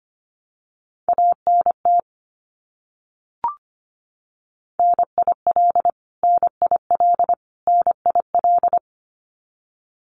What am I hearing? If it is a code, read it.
Morse code, heard as a keyed tone, about 25 wpm, 700 Hz. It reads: ANT NIL NIL NIL
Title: Abbreviations Repeated 3x - 25wpm